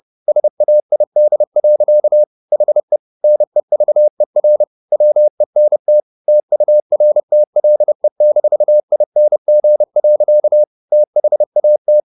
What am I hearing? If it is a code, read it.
Morse code, heard as a keyed tone, about 30 wpm, 600 Hz. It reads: SAID. HE NEVER WENT TURTLE-ING. THAT
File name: the old man and the sea / the old man and the sea 1 30wpm